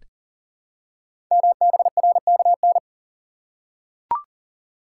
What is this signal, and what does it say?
Morse code, heard as a keyed tone, about 38 wpm, 700 Hz. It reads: MBRKN
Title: Code Groups - 5 Letters - 40wpm